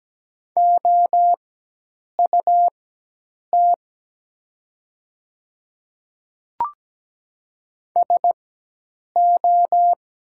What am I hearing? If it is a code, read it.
Morse code, heard as a keyed tone, about 17 wpm, spaced as for 8 wpm, 700 Hz. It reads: OUT SO
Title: 4x Character Spacing - Top 100 Words - 17wpm